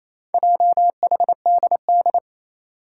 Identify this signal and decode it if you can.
Morse code, heard as a keyed tone, about 27 wpm, 700 Hz. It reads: JHDD
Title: Four Challenging Letter-Number Combinations 28wpm